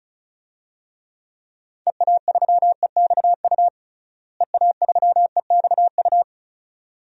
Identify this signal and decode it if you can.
Morse code, heard as a keyed tone, about 36 wpm, 700 Hz. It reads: EA3EXU EA3EXU
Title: DXCC-100 Call Signs - 6 Characters - Repeated 2x - 35wpm